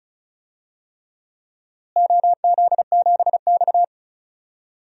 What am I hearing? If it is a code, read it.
Morse code, heard as a keyed tone, about 36 wpm, 700 Hz. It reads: OZ7X
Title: DXCC-100 Call Signs - 4 Characters - Repeated 3x - 35wpm